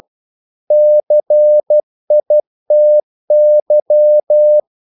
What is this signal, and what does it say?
Morse code, heard as a keyed tone, about 12 wpm, 600 Hz. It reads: CITY
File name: CW / Practice / NationalCapitals12